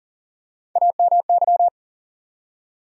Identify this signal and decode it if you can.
Morse code, heard as a keyed tone, about 38 wpm, 700 Hz. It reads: AMY
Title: USA Common Names - 40wpm